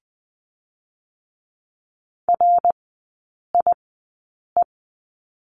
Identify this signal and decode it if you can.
Morse code, heard as a keyed tone, about 20 wpm, spaced as for 8 wpm, 700 Hz. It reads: RIE
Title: 053 Lesson 10e - Letter r - Three Characters 20wpm